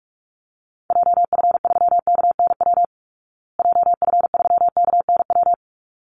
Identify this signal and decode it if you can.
Morse code, heard as a keyed tone, about 44 wpm, 700 Hz. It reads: JF3KNW JF3KNW
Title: Call Signs Repeated 3x - 45wpm